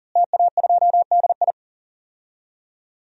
Morse code, heard as a keyed tone, about 40 wpm, 700 Hz. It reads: TA2DI
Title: DXCC-100 Call Signs - 5 Characters - Repeated 2x - 40wpm